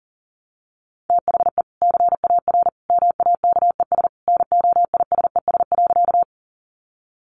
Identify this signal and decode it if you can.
Morse code, heard as a keyed tone, about 40 wpm, 700 Hz. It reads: THE CAR MAKES NOISES.
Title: Easy Phrases 40wpm